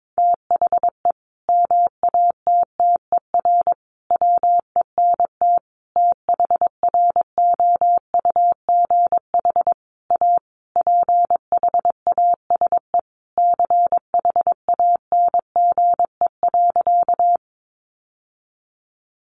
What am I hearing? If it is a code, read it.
Morse code, heard as a keyed tone, about 22 wpm, 700 Hz. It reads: THE MATTER WENT THROUGH A PHASE CHANGE.
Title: Sentences from Top 1200 Words 22wpm